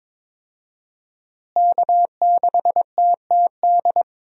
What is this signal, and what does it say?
Morse code, heard as a keyed tone, about 22 wpm, 700 Hz. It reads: K6TTD
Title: USA Call Signs - 5 Characters - Repeated 2x - 22wpm